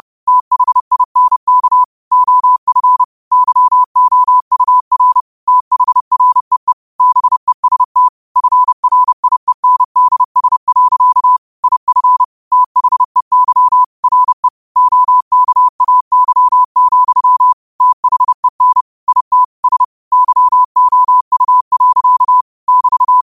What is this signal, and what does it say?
Morse code, heard as a keyed tone, about 30 wpm, 1000 Hz. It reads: THINK OF YOUR THREE BEST FRIENDS. IF THEY RE OKAY, THEN IT S YOU. <BT>